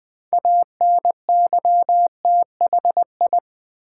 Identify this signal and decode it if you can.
Morse code, heard as a keyed tone, about 20 wpm, 700 Hz. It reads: ANYTHI
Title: Build A Word - 2 Letters At A Time - 8-Letter Words - 20wpm